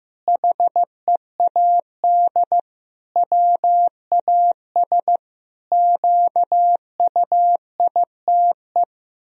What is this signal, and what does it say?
Morse code, heard as a keyed tone, about 15 wpm, 700 Hz. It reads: HEAD WAS QUITE
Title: Sentences from Top 1000 Words - Review Part 2 - 15wpm